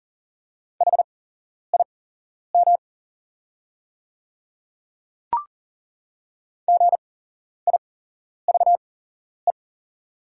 Morse code, heard as a keyed tone, about 40 wpm, spaced as for 11 wpm, 700 Hz. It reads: HIM GIVE
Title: 8x Character Spacing - Top 300 Words - 40wpm